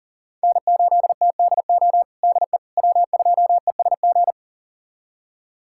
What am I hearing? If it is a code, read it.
Morse code, heard as a keyed tone, about 40 wpm, 700 Hz. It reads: N8TDO DE W2ESG
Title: USA Callsign DE Callsign - 5 Characters - 40wpm